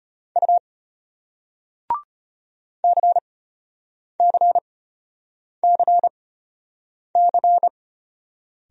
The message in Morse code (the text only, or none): U CCCC